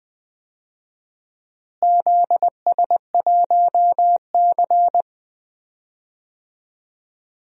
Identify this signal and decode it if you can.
Morse code, heard as a keyed tone, about 20 wpm, 700 Hz. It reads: ZS1C
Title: DXCC-100 Call Signs - 4 Characters - Repeated 3x - 20wpm